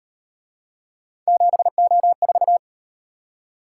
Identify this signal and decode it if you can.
Morse code, heard as a keyed tone, about 38 wpm, 700 Hz. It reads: 7O4